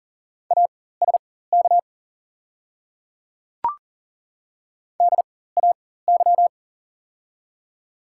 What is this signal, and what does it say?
Morse code, heard as a keyed tone, about 39 wpm, spaced as for 19 wpm, 700 Hz. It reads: ASK DAY
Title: 4x Character Spacing - Top 200 Words - 40wpm